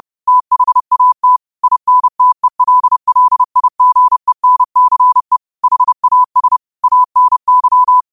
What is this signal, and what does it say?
Morse code, heard as a keyed tone, about 30 wpm, 1000 Hz. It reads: THAT INTELLIGENCE HAS ANY
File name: qotdfiles / 30wpm